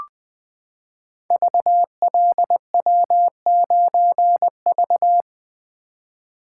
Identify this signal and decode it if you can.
Morse code, heard as a keyed tone, about 20 wpm, 700 Hz. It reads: VLW9V
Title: Code Groups - 5 Letter:Numbers - 20wpm